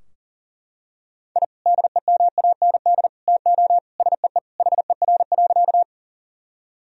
Morse code, heard as a keyed tone, about 40 wpm, 700 Hz. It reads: I DEMAND TO SEE HER.
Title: Sentences from Top 1000 Words - Review Part 4 - 40wpm